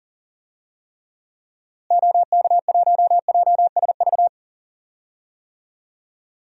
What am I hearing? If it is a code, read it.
Morse code, heard as a keyed tone, about 40 wpm, 700 Hz. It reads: OK1JSV
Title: DXCC-100 Call Signs - 6 Characters - Repeated 2x - 40wpm